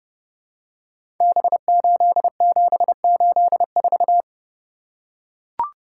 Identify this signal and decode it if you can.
Morse code, heard as a keyed tone, about 29 wpm, 700 Hz. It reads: B8784
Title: Code Groups - 5 Letter:Numbers - 30wpm